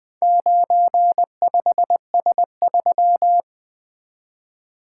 Morse code, heard as a keyed tone, about 23 wpm, 700 Hz. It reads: 95S3
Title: Speed-Racing - Four Letter-Number Combinations - 20wpm